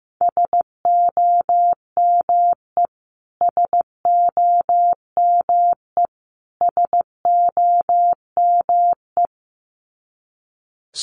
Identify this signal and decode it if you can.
Morse code, heard as a keyed tone, about 15 wpm, 700 Hz. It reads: SOME SOME SOME
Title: Top 200 Words Repeated 15wpm